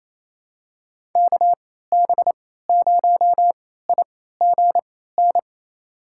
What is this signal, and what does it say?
Morse code, heard as a keyed tone, about 28 wpm, spaced as for 16 wpm, 700 Hz. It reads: KB0IGN
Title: USA Call Signs - 3x Character Spacing - 28wpm